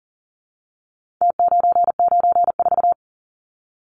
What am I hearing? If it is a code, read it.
Morse code, heard as a keyed tone, about 40 wpm, 700 Hz. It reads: T994
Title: Numbers - Contest Style - 4-Digit 40wpm